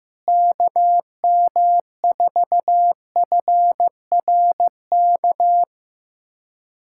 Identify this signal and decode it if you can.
Morse code, heard as a keyed tone, about 15 wpm, 700 Hz. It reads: KM4FRK
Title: USA Callsign DE Callsign 15wpm